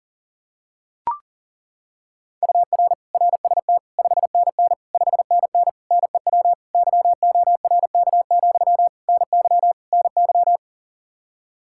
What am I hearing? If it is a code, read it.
Morse code, heard as a keyed tone, about 40 wpm, 700 Hz. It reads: UR RST 5NN 5NN NEW YORK, NY NY
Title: Practice QSO - Exchange 1 - Beginning - 40wpm